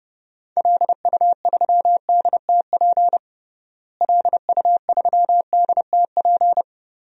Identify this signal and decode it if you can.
Morse code, heard as a keyed tone, about 32 wpm, 700 Hz. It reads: LU3DTP LU3DTP
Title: DXCC-100 Call Signs - 6 Characters - Repeated 3x - 30wpm